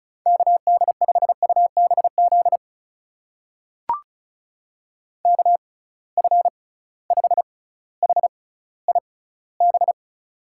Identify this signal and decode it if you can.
Morse code, heard as a keyed tone, about 36 wpm, spaced as for 18 wpm, 700 Hz. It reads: KD5UBZ KF5HIB
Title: USA Call Signs - 6 Characters - 6x Character Spacing - 35wpm